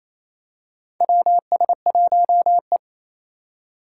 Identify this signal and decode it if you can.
Morse code, heard as a keyed tone, about 28 wpm, 700 Hz. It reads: WS1E
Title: USA Call Signs - 4 Characters - 3x Character Spacing - 28wpm